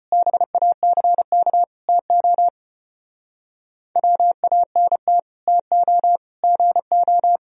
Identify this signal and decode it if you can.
Morse code, heard as a keyed tone, about 33 wpm, 700 Hz. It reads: BACK TO WANT TO GO